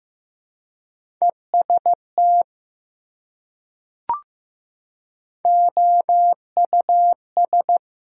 Morse code, heard as a keyed tone, about 15 wpm, 700 Hz. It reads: EST OUS
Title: Prefixes and Suffixes 15wpm